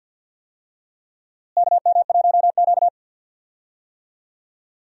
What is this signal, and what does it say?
Morse code, heard as a keyed tone, about 52 wpm, 700 Hz. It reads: KM1Q
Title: USA Call Signs - 4 Characters - Repeated 3x - 50wpm